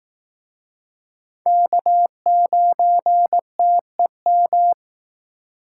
Morse code, heard as a keyed tone, about 19 wpm, 700 Hz. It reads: K9TEM